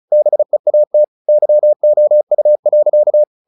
Morse code, heard as a keyed tone, about 34 wpm, 600 Hz. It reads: BEAT YOU.
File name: the old man and the sea 2 35wpm